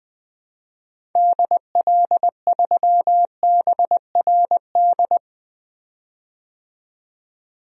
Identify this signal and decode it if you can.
Morse code, heard as a keyed tone, about 20 wpm, 700 Hz. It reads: DL3BRD
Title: DXCC-100 Call Signs - 6 Characters - 20wpm